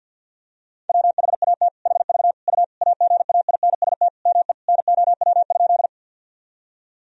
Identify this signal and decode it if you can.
Morse code, heard as a keyed tone, about 52 wpm, 700 Hz. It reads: WHAT HV U AGAINST ME NOW?
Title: Sentences from Top 400 Words and Abbreviations 50wpm